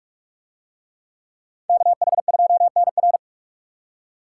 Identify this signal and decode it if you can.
Morse code, heard as a keyed tone, about 44 wpm, 700 Hz. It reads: KH2NR